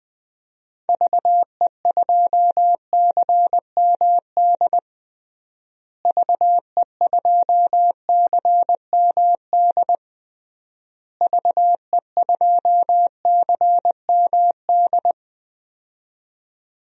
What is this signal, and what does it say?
Morse code, heard as a keyed tone, about 20 wpm, 700 Hz. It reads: VE2CMD VE2CMD VE2CMD
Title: DXCC-100 Call Signs - 6 Characters - Repeated 3x - 20wpm